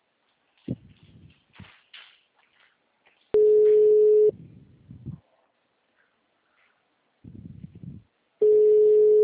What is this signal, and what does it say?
Ringback tone (Europe, 425 Hz).